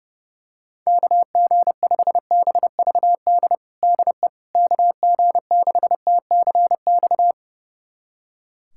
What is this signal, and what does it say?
Morse code, heard as a keyed tone, about 30 wpm, 700 Hz. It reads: KG5BVD DE KG6TCX
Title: USA Callsign DE Callsign 30wpm